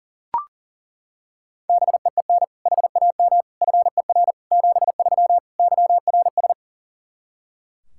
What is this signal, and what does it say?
Morse code, heard as a keyed tone, about 40 wpm, 700 Hz. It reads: BEEN HAM FER 73 YRS
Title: Practice QSO - Exchange 3 - More Info - 40wpm